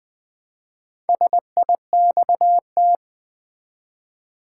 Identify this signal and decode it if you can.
Morse code, heard as a keyed tone, about 20 wpm, 700 Hz. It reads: SIXT
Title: Build A Word - 1 Letter At A Time - 5-Letter Words - 20wpm